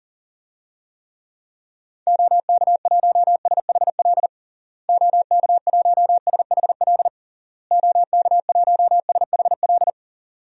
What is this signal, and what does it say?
Morse code, heard as a keyed tone, about 37 wpm, 700 Hz. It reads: OK1SHL OK1SHL OK1SHL
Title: DXCC-100 Call Signs - 6 Characters - Repeated 3x - 40wpm